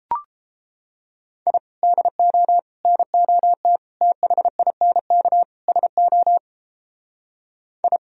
Morse code, heard as a keyed tone, about 33 wpm, 700 Hz. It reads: I DO NOT THINK SO I